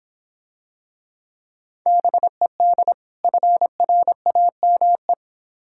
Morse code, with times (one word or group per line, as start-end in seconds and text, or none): 1.9-2.9 BED
3.2-5.1 FRAME